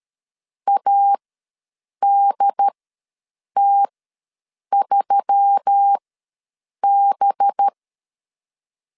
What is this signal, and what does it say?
Morse code, heard as a keyed tone, about 13 wpm, spaced as for 7 wpm, 800 Hz. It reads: ADT3B
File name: 5 LB ws 12wpm